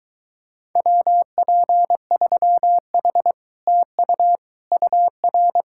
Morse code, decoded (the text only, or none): WP3H TU UR